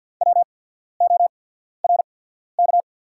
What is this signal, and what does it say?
Morse code, heard as a keyed tone, about 48 wpm, spaced as for 14 wpm, 700 Hz. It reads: WORK
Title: 8x Character Spacing - Top 500 Words - 50wpm